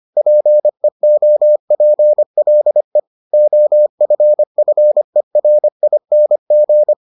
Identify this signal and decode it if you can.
Morse code, heard as a keyed tone, about 25 wpm, 600 Hz. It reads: PEOPLE OFFERING